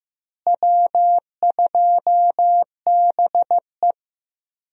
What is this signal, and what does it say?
Morse code, heard as a keyed tone, about 15 wpm, 700 Hz. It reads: W2BE
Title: USA Call Signs - 4 Characters - Repeated 2x - 15wpm